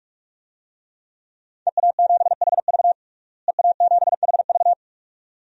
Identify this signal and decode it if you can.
Morse code, heard as a keyed tone, about 46 wpm, 700 Hz. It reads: EA7HV EA7HV
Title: DXCC-100 Call Signs - 5 Characters - Repeated 2x - 45wpm